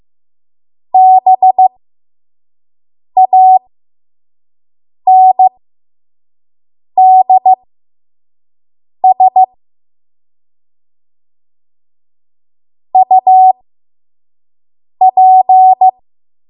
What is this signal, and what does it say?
Morse code, heard as a keyed tone, about 15 wpm, spaced as for 5 wpm, 750 Hz. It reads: BANDS UP
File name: Morse / 05WPM